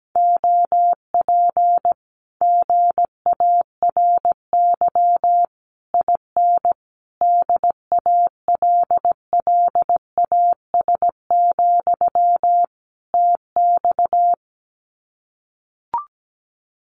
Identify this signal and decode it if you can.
Morse code, heard as a keyed tone, about 17 wpm, 700 Hz. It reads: OP GARY IN DALLAS, TX
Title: Common QSO Phrases Encore 17wpm